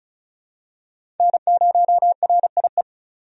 Morse code, heard as a keyed tone, about 36 wpm, 700 Hz. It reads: N0RIE
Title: USA Call Signs - 5 Characters - 35wpm